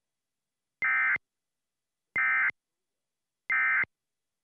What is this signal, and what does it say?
Emergency Alert System (SAME) tones, heard as harsh buzzing bursts. It is heard from 0.8 to 3.8 s.